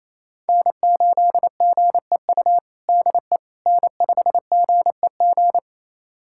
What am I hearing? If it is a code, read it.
Morse code, heard as a keyed tone, about 28 wpm, 700 Hz. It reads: N8GEU DE N5GEG
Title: USA Callsign DE Callsign - 5 Characters - 28wpm